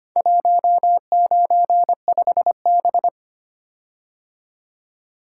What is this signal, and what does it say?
Morse code, heard as a keyed tone, about 25 wpm, 690 Hz. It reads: 195B
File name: practice / 25wpm / Four Letter-Number Combinations - 25wpm